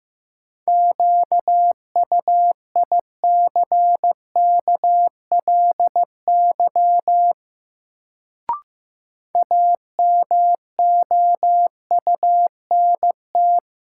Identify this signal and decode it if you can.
Morse code, heard as a keyed tone, about 15 wpm, 700 Hz. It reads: QUICKLY AMOUNT
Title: Sentences from Top 700 Words - New Word Drill - 15wpm